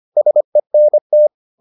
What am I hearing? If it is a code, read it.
Morse code, heard as a keyed tone, about 25 wpm, 600 Hz. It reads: SENT